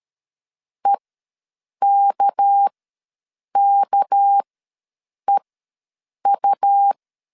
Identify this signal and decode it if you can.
Morse code, heard as a keyed tone, about 13 wpm, spaced as for 7 wpm, 800 Hz. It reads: EKKEU
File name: LEZIONI / 6 PKX0 / 6 PK ws 12wpm 3116